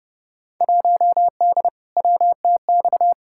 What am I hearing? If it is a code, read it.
Morse code, heard as a keyed tone, about 31 wpm, 700 Hz. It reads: 1D WTX
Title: Field Day Exchanges - 30wpm